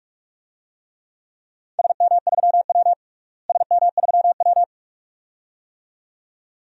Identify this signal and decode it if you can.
Morse code, heard as a keyed tone, about 46 wpm, 700 Hz. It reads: SM3W SM3W
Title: DXCC-100 Call Signs - 4 Characters - Repeated 2x - 45wpm